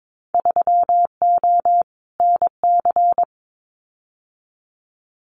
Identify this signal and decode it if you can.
Morse code, heard as a keyed tone, about 22 wpm, 700 Hz. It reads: 3O NC
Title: Winter Field Day Exchanges 22wpm